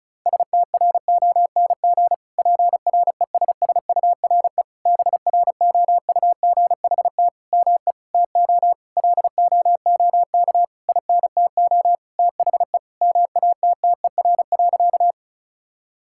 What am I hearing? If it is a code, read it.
Morse code, heard as a keyed tone, about 36 wpm, 700 Hz. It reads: STRONG PRESSURE BROUGHT ME TO LOOK INTO THE MATTER.
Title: Sentences from Top 1000 Words - Review Part 6 - 35wpm